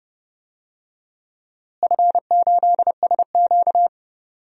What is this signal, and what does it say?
Morse code, heard as a keyed tone, about 31 wpm, 700 Hz. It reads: F8SQ
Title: DXCC-100 Call Signs - 4 Characters - Repeated 2x - 30wpm